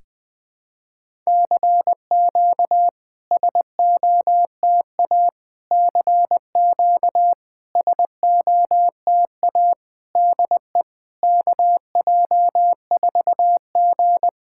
Morse code, heard as a keyed tone, about 20 wpm, 690 Hz. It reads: CQ SOTA CQ SOTA DE KJ4G
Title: SOTA QSO Practice 20wpm